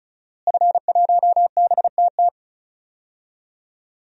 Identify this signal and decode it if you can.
Morse code, heard as a keyed tone, about 36 wpm, 700 Hz. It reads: F1BTT
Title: DXCC-100 Call Signs - 5 Characters - Repeated 3x - 35wpm